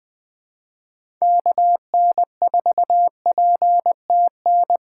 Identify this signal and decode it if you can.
Morse code, heard as a keyed tone, about 20 wpm, 700 Hz. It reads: KN4PTN
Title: USA Call Signs - 6 Characters - 2x Character Spacing - 20wpm